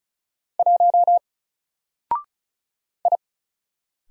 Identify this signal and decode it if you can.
Morse code, heard as a keyed tone, about 36 wpm, 700 Hz. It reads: 1 I
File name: Single-Letter-Number-35wpm